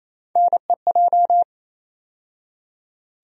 Morse code, heard as a keyed tone, about 28 wpm, 700 Hz. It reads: NEJ
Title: USA Call Sign Postfixes - 3 Letters 28wpm